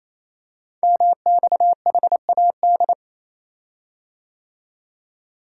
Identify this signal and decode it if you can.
Morse code, heard as a keyed tone, about 27 wpm, 700 Hz. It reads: MXHAD
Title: Code Groups - 5 Letters - 28wpm